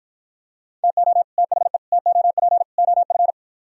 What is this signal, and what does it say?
Morse code, heard as a keyed tone, about 52 wpm, 700 Hz. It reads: TO THE TOP OF